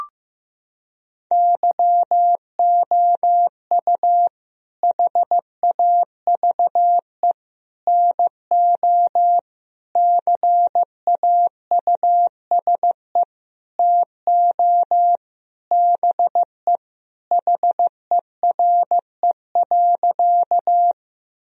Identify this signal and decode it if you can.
Morse code, heard as a keyed tone, about 16 wpm, 700 Hz. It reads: YOU HAVE NO CAUSE TO BE HERE.
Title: Sentences from Top 1000 Words - Review Part 1 - 15wpm